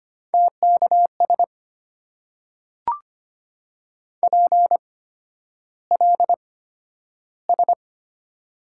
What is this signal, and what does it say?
Morse code, heard as a keyed tone, about 25 wpm, spaced as for 11 wpm, 700 Hz. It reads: TKS PLS
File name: Top 90 QSO Elements - 8x Character Spacing - 25wpm